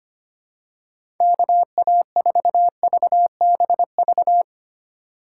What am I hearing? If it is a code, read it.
Morse code, heard as a keyed tone, about 25 wpm, 700 Hz. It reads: KA4VBV
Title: USA Call Signs - 6 Characters - 3x Character Spacing - 25wpm